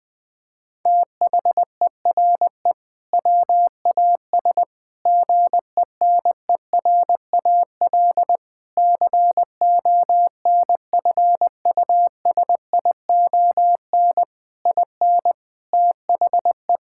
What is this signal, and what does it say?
Morse code, heard as a keyed tone, about 20 wpm, 700 Hz. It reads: THERE WAS GENERAL CONFUSION IN THE
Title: Sentences from Top 1600 Words 20wpm